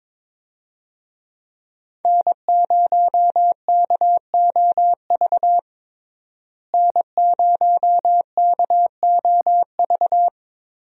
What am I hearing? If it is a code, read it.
Morse code, heard as a keyed tone, about 22 wpm, 700 Hz. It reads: N0KOV N0KOV
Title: USA Call Signs - 5 Characters - Repeated 3x - 22wpm